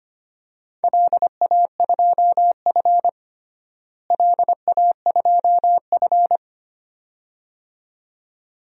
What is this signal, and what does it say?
Morse code, heard as a keyed tone, about 25 wpm, 700 Hz. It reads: LA2F LA2F
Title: DXCC-100 Call Signs - 4 Characters - Repeated 2x - 25wpm